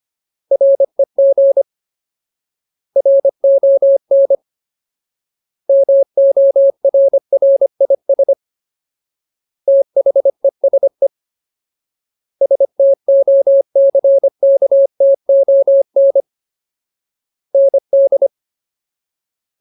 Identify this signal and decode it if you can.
Morse code, heard as a keyed tone, about 25 wpm, 550 Hz. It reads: REG RON MORRIS THESE STOCKTON ND
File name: Ultimate Word List / 25WPM Ultimate